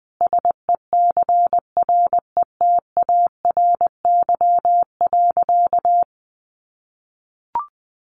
Morse code, heard as a keyed tone, about 20 wpm, 700 Hz. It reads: SECRETARY.
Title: Sentences from Top 1300 Words 20wpm